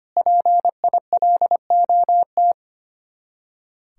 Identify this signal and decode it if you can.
Morse code, heard as a keyed tone, about 32 wpm, 700 Hz. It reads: PILOT